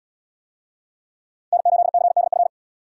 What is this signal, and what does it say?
Morse code, heard as a keyed tone, about 78 wpm, 700 Hz. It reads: N8CDF